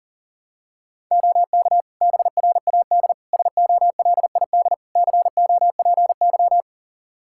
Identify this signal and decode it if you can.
Morse code, heard as a keyed tone, about 40 wpm, 700 Hz. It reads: OK BRAD SOLID COPY
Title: Practice QSO - Exchange 3 - More Info - 40wpm